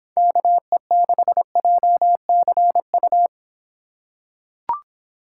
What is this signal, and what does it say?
Morse code, heard as a keyed tone, about 25 wpm, 700 Hz. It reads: KE6JCU